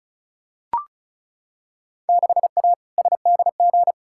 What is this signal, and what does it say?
Morse code, heard as a keyed tone, about 34 wpm, 700 Hz. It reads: 6A SDG
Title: Field Day Exchanges Encore - 35wpm